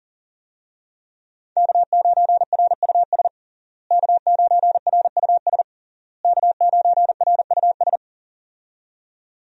Morse code, heard as a keyed tone, about 40 wpm, 700 Hz. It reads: K9RUS K9RUS K9RUS
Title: USA Call Signs - 5 Characters - Repeated 3x - 40wpm